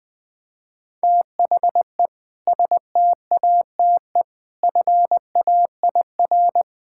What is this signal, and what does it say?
Morse code, heard as a keyed tone, about 20 wpm, 700 Hz. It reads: THE STATE FAIR
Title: Sentences with 2-Word Phrases - 20wpm